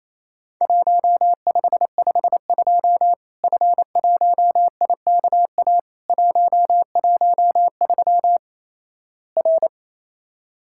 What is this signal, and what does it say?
Morse code, heard as a keyed tone, about 28 wpm, 700 Hz. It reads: 1552 F1IKA 113 R
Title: Worked All Europe DX Contest - QTC Exchange - 28wpm